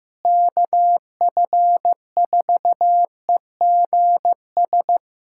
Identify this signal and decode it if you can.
Morse code, heard as a keyed tone, about 15 wpm, 700 Hz. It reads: KF4EGS
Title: USA Callsign DE Callsign - 6 Characters - 15wpm